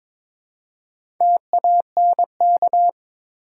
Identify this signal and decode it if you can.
Morse code, heard as a keyed tone, about 22 wpm, 700 Hz. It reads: TANK
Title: Four Letter Words - 22wpm